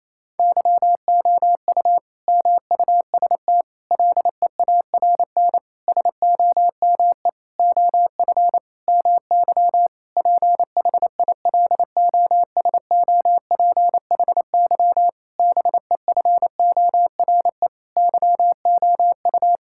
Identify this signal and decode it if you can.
Morse code, heard as a keyed tone, about 28 wpm, 700 Hz. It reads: YOU MUST LEARN SOME OF MY PHILOSOPHY BEFORE YOU
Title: Sentences from Top 1000 Words - Review Part 6 - 28wpm